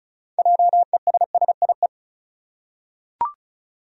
Morse code, heard as a keyed tone, about 34 wpm, 700 Hz. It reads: JESSIE
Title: USA Common Names - 35wpm